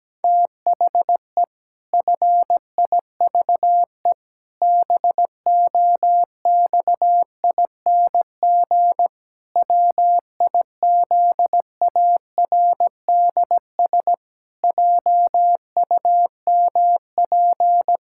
Morse code, heard as a keyed tone, about 17 wpm, 700 Hz. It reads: THE FIVE BOXING WIZARDS JUMP
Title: Pangram Challenge v2 - 17wpm